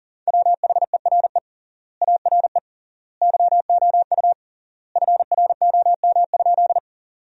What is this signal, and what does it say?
Morse code, heard as a keyed tone, about 40 wpm, 700 Hz. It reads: WHERE ARE YOU FROM?
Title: 3x Word-Spacing - Sentences from Top 200 Words - 40wpm